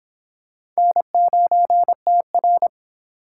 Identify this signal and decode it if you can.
Morse code, heard as a keyed tone, about 25 wpm, 700 Hz. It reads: N9TR